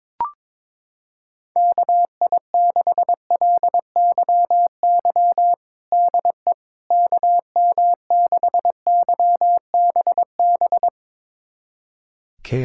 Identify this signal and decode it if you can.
Morse code, heard as a keyed tone, about 22 wpm, 700 Hz. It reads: KI6LYY DE KM6YBB
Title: USA Callsign DE Callsign - 6 Characters - 22wpm